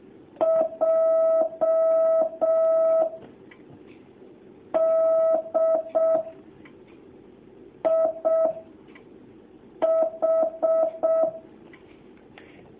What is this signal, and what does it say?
Morse code, heard as a keyed tone, about 6 wpm, 650 Hz. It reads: J D I H
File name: jh8cbh / 2019cw